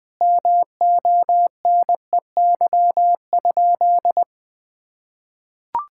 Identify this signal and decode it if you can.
Morse code, heard as a keyed tone, about 21 wpm, 700 Hz. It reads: MONEY?
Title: Sentences from Top 1000 Words - Review Part 5 - 20wpm